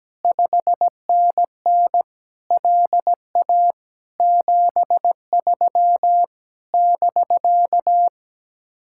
Morse code, heard as a keyed tone, about 17 wpm, 700 Hz. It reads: 5NN LA 73 <BK>
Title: Speed-Racing Common QSO Phrases Encore - 17wpm